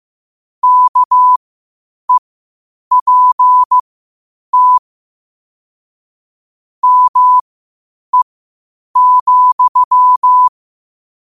Morse code, heard as a keyed tone, about 15 wpm, spaced as for 8 wpm, 1000 Hz. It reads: KEPT ME,
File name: qotdfiles / 8wpm